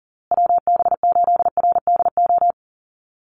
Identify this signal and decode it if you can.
Morse code, heard as a keyed tone, about 40 wpm, 700 Hz. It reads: WB8RDO
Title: USA Call Signs Repeated 2x 40wpm